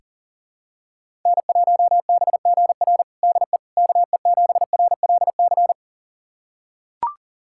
Morse code, heard as a keyed tone, about 40 wpm, 700 Hz. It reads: N1BGR DE KE7RLC
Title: USA Callsign DE Callsign 40wpm